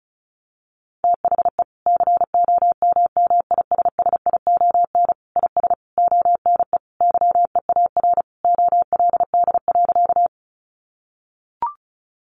Morse code, heard as a keyed tone, about 34 wpm, 700 Hz. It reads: THE COMMISSION IS ONE YEAR OLD.
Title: Sentences from Top 1400 Words 35wpm